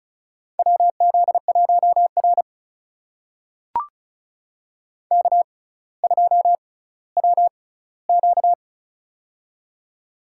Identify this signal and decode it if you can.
Morse code, heard as a keyed tone, about 36 wpm, spaced as for 18 wpm, 700 Hz. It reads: WZ1R K2WQ
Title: USA Call Signs - 4 Characters - 6x Character Spacing - 35wpm